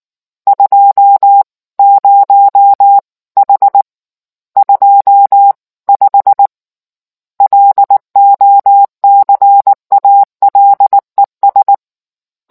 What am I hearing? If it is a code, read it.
Morse code, heard as a keyed tone, about 19 wpm, 800 Hz. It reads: 20H 25 LOCALES